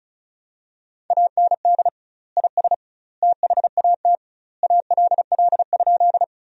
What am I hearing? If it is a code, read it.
Morse code, heard as a keyed tone, about 36 wpm, 700 Hz. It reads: AND IS THAT ALL?
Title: 2x Word-Spacing - Sentences from Top 100 Words - 35wpm